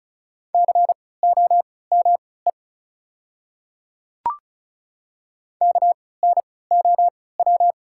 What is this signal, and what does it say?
Morse code, heard as a keyed tone, about 36 wpm, spaced as for 20 wpm, 690 Hz. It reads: COME KNOW
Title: 3x Character Spacing - Top 100 Words - 35wpm